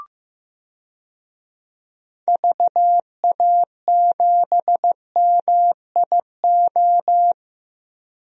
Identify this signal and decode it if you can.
Morse code, heard as a keyed tone, about 15 wpm, 700 Hz. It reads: VA7MIO
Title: DXCC-100 Call Signs - 6 Characters - Repeated 3x - 15wpm